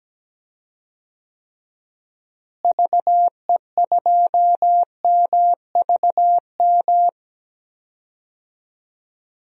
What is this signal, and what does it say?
Morse code, heard as a keyed tone, about 17 wpm, 700 Hz. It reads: VE2MVM
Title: DXCC-100 Call Signs - 6 Characters - Repeated 2x - 17wpm